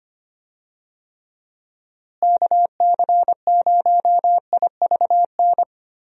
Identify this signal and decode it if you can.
Morse code, heard as a keyed tone, about 25 wpm, 700 Hz. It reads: KC0IVN
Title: USA Call Signs - 6 Characters - Repeated 2x - 25wpm